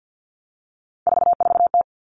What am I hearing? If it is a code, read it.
Morse code, heard as a keyed tone, about 50 wpm, 700 Hz. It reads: 44T
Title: Three Challenging Letter-Number Combinations 50wpm